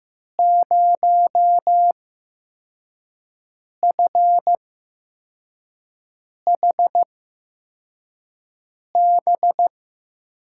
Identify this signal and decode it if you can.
Morse code, heard as a keyed tone, about 15 wpm, spaced as for 4 wpm, 700 Hz. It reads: 0FHB
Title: USA Call Signs - 5 Characters - 8x Character Spacing - 15wpm